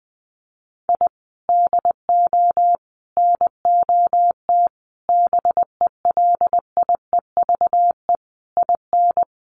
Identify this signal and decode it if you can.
Morse code, heard as a keyed tone, about 20 wpm, 700 Hz. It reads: I DO NOT BELIEVE IN
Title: Sets of 5 Words Encore 20wpm